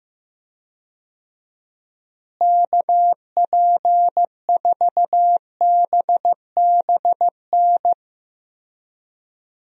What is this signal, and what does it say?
Morse code, heard as a keyed tone, about 15 wpm, 700 Hz. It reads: KP4BBN
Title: DXCC-100 Call Signs - 6 Characters - Repeated 2x - 15wpm